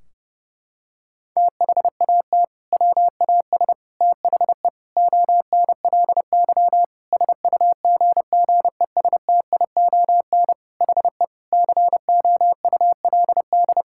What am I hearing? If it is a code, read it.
Morse code, heard as a keyed tone, about 30 wpm, 700 Hz. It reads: THAT WAS THE ONLY SUGGESTION HE COULD
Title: Sentences from Top 1600 Words 30wpm